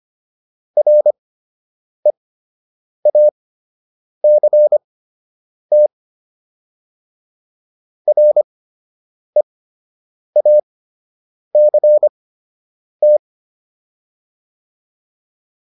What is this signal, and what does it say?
Morse code, heard as a keyed tone, about 25 wpm, spaced as for 8 wpm, 600 Hz. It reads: REACT REACT